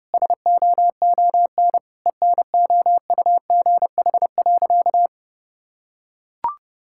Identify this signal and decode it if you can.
Morse code, heard as a keyed tone, about 30 wpm, 700 Hz. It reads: SOON ENOUGH.
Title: Sentences from Top 1000 Words - Review Part 6 - 30wpm